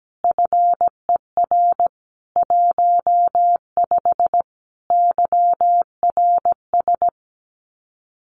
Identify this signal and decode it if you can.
Morse code, heard as a keyed tone, about 17 wpm, 700 Hz. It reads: FER 15 YRS
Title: Common QSO Phrases Encore 17wpm